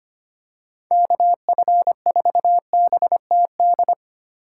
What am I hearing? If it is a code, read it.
Morse code, heard as a keyed tone, about 25 wpm, 700 Hz. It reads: KF4BTD
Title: USA Call Signs - 6 Characters - 3x Character Spacing - 25wpm